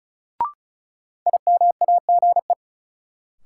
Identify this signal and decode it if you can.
Morse code, heard as a keyed tone, about 36 wpm, 700 Hz. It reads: IMAGE
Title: Sentences from Top 700 Words - New Word Drill - 35wpm